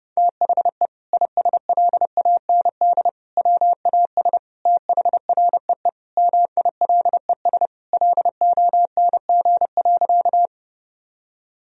Morse code, heard as a keyed tone, about 30 wpm, 700 Hz. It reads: THE ISLAND WAS THREE MILES LONG.
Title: Sentences from Top 1000 Words - Review Part 3 - 30wpm